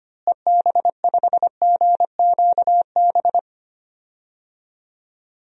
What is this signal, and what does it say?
Morse code, heard as a keyed tone, about 25 wpm, 700 Hz. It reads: EB5GQB
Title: DXCC-100 Call Signs - 6 Characters - Repeated 3x - 25wpm